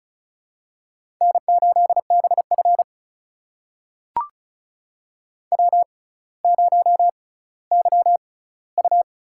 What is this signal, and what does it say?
Morse code, heard as a keyed tone, about 36 wpm, spaced as for 18 wpm, 700 Hz. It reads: N8BF W0YU
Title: USA Call Signs - 4 Characters - 6x Character Spacing - 35wpm